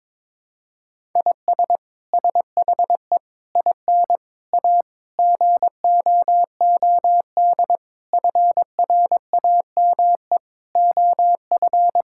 Morse code, heard as a keyed tone, about 22 wpm, 700 Hz. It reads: IS SHE IN A GOOD FRAME OF
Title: Sentences from Top 1000 Words - Review Part 6 - 22wpm